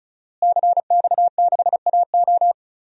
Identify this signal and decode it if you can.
Morse code, heard as a keyed tone, about 36 wpm, 700 Hz. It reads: CX6AO
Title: DXCC-100 Call Signs - 5 Characters - Repeated 3x - 35wpm